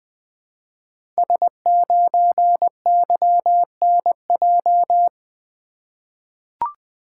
Morse code, heard as a keyed tone, about 20 wpm, 700 Hz. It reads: S9YNJ
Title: Code Groups - 5 Letter:Numbers - 20wpm